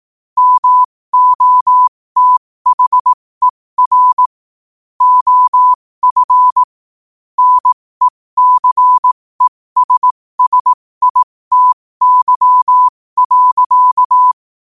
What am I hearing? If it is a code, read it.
Morse code, heard as a keyed tone, about 18 wpm, spaced as for 15 wpm, 1000 Hz. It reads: MOTHER OF NECESSITY.
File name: qotdfiles / 15wpm